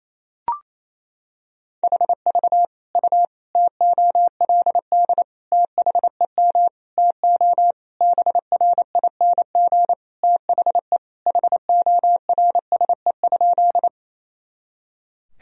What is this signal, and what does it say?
Morse code, heard as a keyed tone, about 28 wpm, 700 Hz. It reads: HV U TOLD THEM TO BRING THE HORSE?
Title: Sentences from Top 500 Words and Abbreviations 28wpm